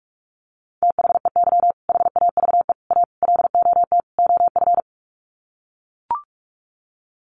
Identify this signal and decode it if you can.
Morse code, heard as a keyed tone, about 44 wpm, 690 Hz. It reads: THEY HAVE A LOT OF
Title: Sets of 5 Words Encore2 45wpm